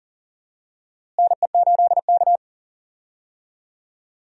Morse code, heard as a keyed tone, about 40 wpm, 700 Hz. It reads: NE8K
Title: USA Call Signs - 4 Characters - Repeated 3x - 40wpm